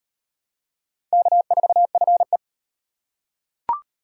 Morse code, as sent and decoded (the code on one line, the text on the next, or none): -.- ....- ..-. .
K4FE